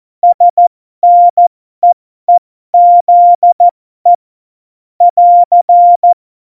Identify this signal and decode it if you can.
Morse code, heard as a keyed tone, about 14 wpm, spaced as for 12 wpm, 700 Hz. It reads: SNEEZE <AR>
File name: Morsekurs / Witze